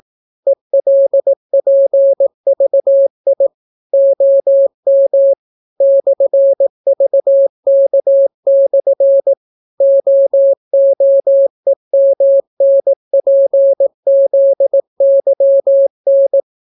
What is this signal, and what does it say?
Morse code, heard as a keyed tone, about 18 wpm, 550 Hz. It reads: ELPVI OM /VK/ OOEMNPZYN